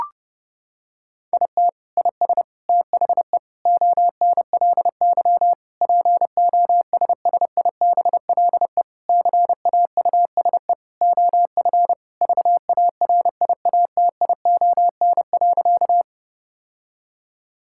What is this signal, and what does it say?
Morse code, heard as a keyed tone, about 32 wpm, 700 Hz. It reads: IT IS THE ONLY POSSIBLE CAUSE OF VARIATION.
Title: Sentences from Top 1600 Words 30wpm